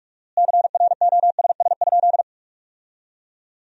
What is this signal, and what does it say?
Morse code, heard as a keyed tone, about 45 wpm, 700 Hz. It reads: CROSS?
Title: 4x Word-Spacing - Sentences from Top 300 Words - 45wpm